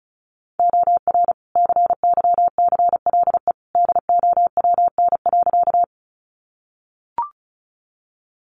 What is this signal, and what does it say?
Morse code, heard as a keyed tone, about 34 wpm, 700 Hz. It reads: OR CYCLE DOWN.
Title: Sentences from Top 900 Words 35wpm